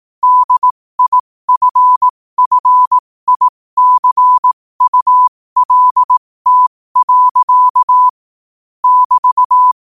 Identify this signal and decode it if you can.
Morse code, heard as a keyed tone, about 18 wpm, spaced as for 15 wpm, 1000 Hz. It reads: DIFFICULT. <BT>
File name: qotdfiles / 15wpm